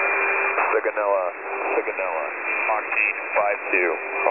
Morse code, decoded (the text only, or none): none